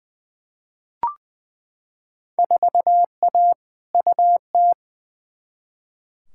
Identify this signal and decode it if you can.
Morse code, heard as a keyed tone, about 20 wpm, 700 Hz. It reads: 4A UT
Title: Field Day Exchanges - 20wpm